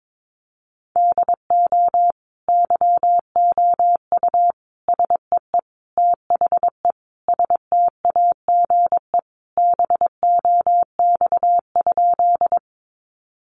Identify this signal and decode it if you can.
Morse code, heard as a keyed tone, about 22 wpm, 700 Hz. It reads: DO YOU SEE THE STAGE BOX?
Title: Sentences from Top 1200 Words 22wpm